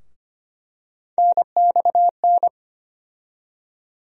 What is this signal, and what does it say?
Morse code, heard as a keyed tone, about 25 wpm, 700 Hz. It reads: NXN
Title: USA Call Sign Postfixes - 3 Letters 25wpm